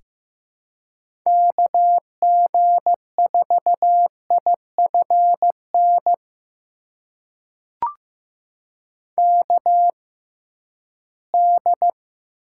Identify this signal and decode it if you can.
Morse code, heard as a keyed tone, about 15 wpm, spaced as for 11 wpm, 700 Hz. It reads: KG4IFN KD
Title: USA Call Signs - 6 Characters - 6x Character Spacing - 15wpm